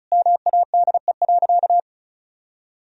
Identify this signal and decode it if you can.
Morse code, heard as a keyed tone, about 34 wpm, 700 Hz. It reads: MADE.
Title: Sentences from Top 1000 Words - Review Part 2 - 35wpm